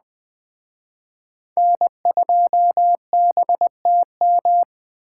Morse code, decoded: N2BTM

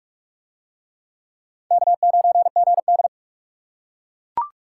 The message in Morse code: K9GD

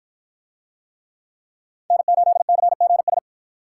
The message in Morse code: N8CGS

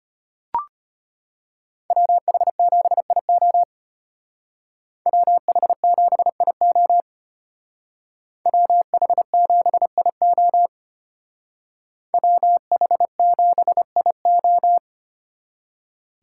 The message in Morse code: WH7IO WH7IO WH7IO WH7IO

